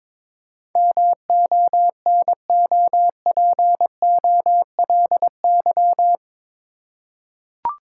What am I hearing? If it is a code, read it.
Morse code, heard as a keyed tone, about 22 wpm, 700 Hz. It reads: MONOPOLY
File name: practice / 22wpm / Eight Letter Words - 22wpm